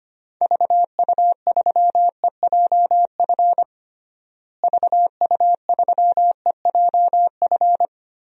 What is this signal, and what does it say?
Morse code, heard as a keyed tone, about 25 wpm, 700 Hz. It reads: VU3EJF VU3EJF
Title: DXCC-100 Call Signs - 6 Characters - Repeated 2x - 25wpm